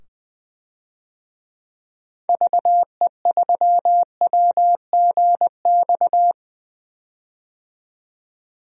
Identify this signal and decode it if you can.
Morse code, heard as a keyed tone, about 20 wpm, 700 Hz. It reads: VE3WGX
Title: DXCC-100 Call Signs - 6 Characters - 20wpm